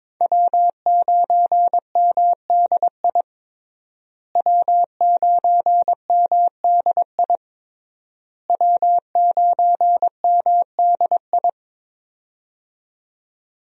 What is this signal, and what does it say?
Morse code, heard as a keyed tone, about 21 wpm, 700 Hz. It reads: W9MDI W9MDI W9MDI
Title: USA Call Signs - 5 Characters - Repeated 3x - 22wpm